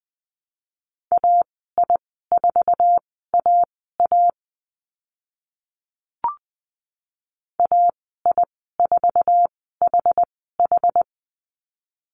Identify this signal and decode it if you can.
Morse code, heard as a keyed tone, about 20 wpm, spaced as for 15 wpm, 700 Hz. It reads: AI4AA AI4HH
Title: 062 Lesson 11g - Letter h - Call Signs 20wpm